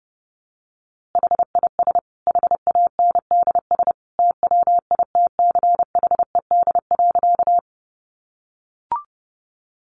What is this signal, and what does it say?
Morse code, heard as a keyed tone, about 30 wpm, 700 Hz. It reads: HIS HANDS TWITCHED.
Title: A Sound Of Thunder 30wpm